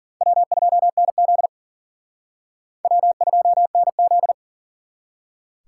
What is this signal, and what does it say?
Morse code, heard as a keyed tone, about 50 wpm, 700 Hz. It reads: W2NZ W2NZ